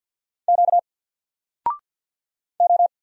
Morse code, heard as a keyed tone, about 50 wpm, 700 Hz. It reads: Q O
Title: Single Character (Letter:Number:Punctuation:Prosign) 50wpm